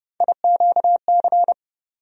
Morse code, heard as a keyed tone, about 31 wpm, 700 Hz. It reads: IQC
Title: USA Call Sign Postfixes - 3 Letters 30wpm